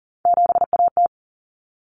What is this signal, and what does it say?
Morse code, heard as a keyed tone, about 40 wpm, 700 Hz. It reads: 7AT
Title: Common Cut Numbers - 3-Digit 40wpm